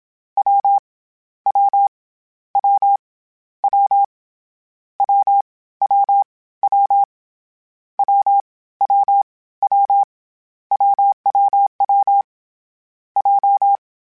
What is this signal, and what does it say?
Morse code, heard as a keyed tone, about 27 wpm, 800 Hz. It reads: W W W W W W W W W W WWW J
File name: Difficult Combos / Difficult Combos-mp3s / AWJ1 @25